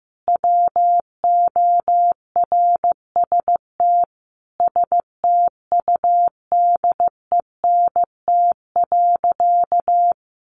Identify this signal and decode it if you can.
Morse code, heard as a keyed tone, about 15 wpm, 700 Hz. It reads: WORST STUDENT.
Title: Additional Easy Phrases 15wpm